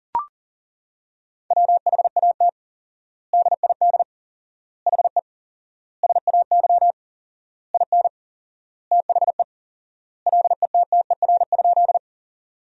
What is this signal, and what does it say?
Morse code, heard as a keyed tone, about 40 wpm, 700 Hz. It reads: WHAT DID HE SAY IN THE LETTER?
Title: 4x Word-Spacing - Sentences from Top 300 Words - 40wpm